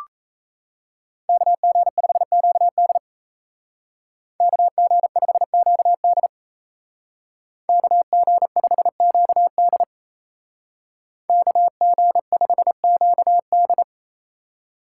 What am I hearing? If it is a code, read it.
Morse code, heard as a keyed tone, about 36 wpm, 700 Hz. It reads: KG5QD KG5QD KG5QD KG5QD